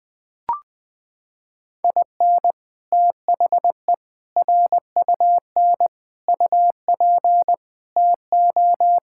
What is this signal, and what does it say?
Morse code, heard as a keyed tone, about 21 wpm, 700 Hz. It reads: IN THE RUN UP TO
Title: Sets of 6 Words - 20wpm